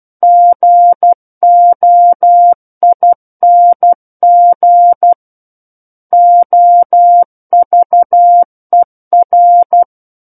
Morse code, heard as a keyed tone, about 12 wpm, 700 Hz. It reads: GOING OVER
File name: Book One / 03-Chapter 3 - 12 WPM